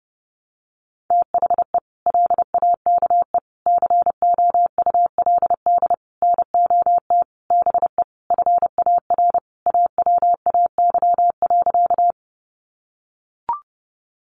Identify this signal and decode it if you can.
Morse code, heard as a keyed tone, about 30 wpm, 700 Hz. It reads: THE LAKE COULD NOT BE FAR AWAY.
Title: Sentences from Top 1100 Words 30wpm